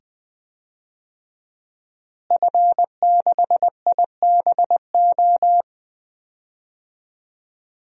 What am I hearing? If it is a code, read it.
Morse code, heard as a keyed tone, about 20 wpm, 700 Hz. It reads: F6IBO
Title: DXCC-100 Call Signs - 5 Characters - Repeated 2x - 20wpm